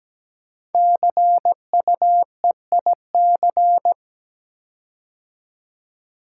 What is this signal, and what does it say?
Morse code, heard as a keyed tone, about 17 wpm, 700 Hz. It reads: CUEIC
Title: Code Groups - 5 Letter:Numbers - 17wpm